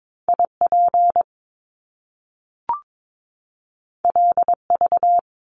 Morse code, heard as a keyed tone, about 22 wpm, 700 Hz. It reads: IP LV
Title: Top Two Letter Combinations in Words 22wpm